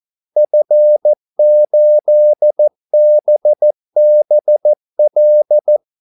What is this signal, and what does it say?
Morse code, heard as a keyed tone, about 14 wpm, 600 Hz. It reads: F8BBL
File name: Callsigns-14wpm